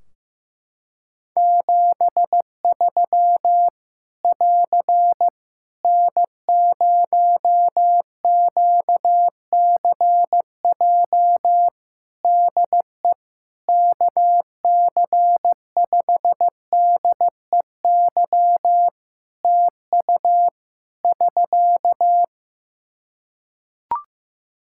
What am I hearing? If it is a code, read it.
Morse code, heard as a keyed tone, about 15 wpm, 700 Hz. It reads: 73 <AR> N0QCJ DE KC5DEY TU <SK>
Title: Practice QSO - Exchange 4 - Ending - 15wpm